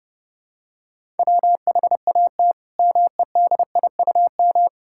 Morse code, heard as a keyed tone, about 30 wpm, 700 Hz. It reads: WHAT MEDIUM
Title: Sentences from Top 1000 Words - Review Part 6 - 30wpm